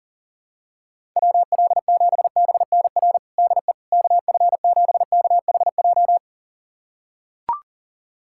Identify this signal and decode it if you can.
Morse code, heard as a keyed tone, about 40 wpm, 700 Hz. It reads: WL7BNR DE KF7KHJ
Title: USA Callsign DE Callsign - 6 Characters - 40wpm